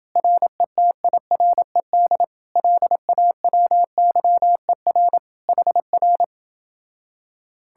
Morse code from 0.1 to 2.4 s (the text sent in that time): RETIRED